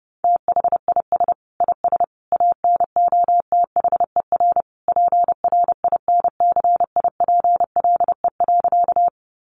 Morse code, heard as a keyed tone, about 30 wpm, 700 Hz. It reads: THIS IS ANOTHER PRINCIPLE.
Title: Sentences from Top 1400 Words 30wpm